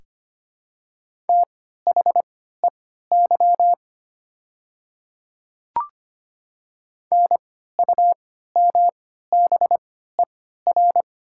Morse code, heard as a keyed tone, about 25 wpm, spaced as for 14 wpm, 700 Hz. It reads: THEY NUMBER
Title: 3x Character Spacing - Top 300 Words - 25wpm